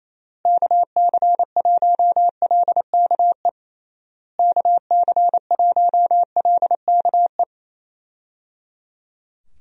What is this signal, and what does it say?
Morse code, heard as a keyed tone, about 27 wpm, 700 Hz. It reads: KC1LKE KC1LKE
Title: USA Call Signs - 6 Characters - Repeated 3x - 28wpm